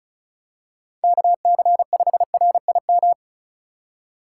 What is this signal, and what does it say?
Morse code, heard as a keyed tone, about 36 wpm, 700 Hz. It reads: KC5RIM
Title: USA Call Signs - 3x Character Spacing - 35wpm